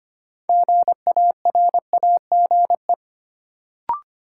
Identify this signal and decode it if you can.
Morse code, heard as a keyed tone, about 25 wpm, 700 Hz. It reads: GARAGE
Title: Sentences from Top 1600 Words - New Word Drill Alt - 25wpm